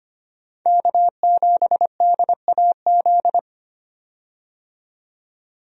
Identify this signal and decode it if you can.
Morse code, heard as a keyed tone, about 25 wpm, 700 Hz. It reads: K7DAZ
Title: USA Call Signs - 5 Characters - 25wpm